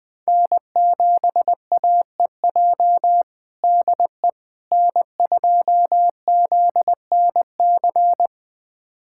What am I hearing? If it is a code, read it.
Morse code, heard as a keyed tone, about 21 wpm, 700 Hz. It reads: N7AEJ DE N2ZNC
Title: USA Callsign DE Callsign 20wpm